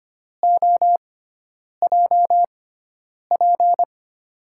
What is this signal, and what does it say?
Morse code, heard as a keyed tone, about 25 wpm, spaced as for 9 wpm, 700 Hz. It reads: OJP
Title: USA Call Signs - 5 Characters - 6x Character Spacing - 25wpm